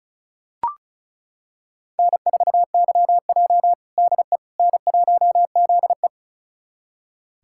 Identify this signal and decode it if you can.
Morse code, heard as a keyed tone, about 34 wpm, 700 Hz. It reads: N4YJ DE N1ZE
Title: USA Callsign DE Callsign - 4 Characters - 35wpm